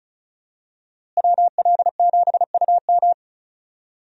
Morse code, heard as a keyed tone, about 36 wpm, 700 Hz. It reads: WL7UM
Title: USA Call Signs - 5 Characters - Repeated 3x - 35wpm